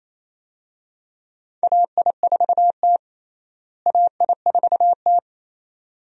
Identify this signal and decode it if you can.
Morse code, heard as a keyed tone, about 28 wpm, 700 Hz. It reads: AI4T AI4T
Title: USA Call Signs - 4 Characters - Repeated 2x - 28wpm